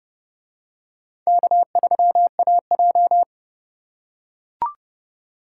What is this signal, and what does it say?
Morse code, heard as a keyed tone, about 28 wpm, 700 Hz. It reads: K3AJ